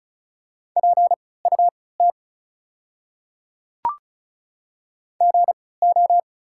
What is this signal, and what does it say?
Morse code, heard as a keyed tone, about 36 wpm, spaced as for 20 wpm, 700 Hz. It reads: PUT GO
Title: 3x Character Spacing - Top 200 Words - 35wpm